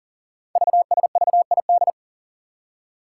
Morse code, heard as a keyed tone, about 42 wpm, 690 Hz. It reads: VSVID
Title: Code Groups - 5 Letters - 40wpm